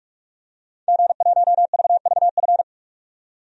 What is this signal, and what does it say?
Morse code, heard as a keyed tone, about 41 wpm, 700 Hz. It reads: G1VVF
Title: DXCC-100 Call Signs - 5 Characters - 45wpm